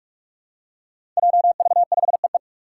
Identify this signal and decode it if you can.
Morse code, heard as a keyed tone, about 41 wpm, 700 Hz. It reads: JV5EE
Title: Five Challenging Letter-Number Combinations 45wpm